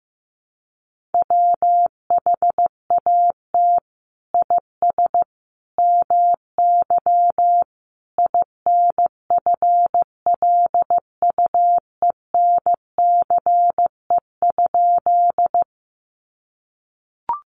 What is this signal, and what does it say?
Morse code, heard as a keyed tone, about 15 wpm, 700 Hz. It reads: WHAT IS MY INFLUENCE?
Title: Sentences from Top 1100 Words 15wpm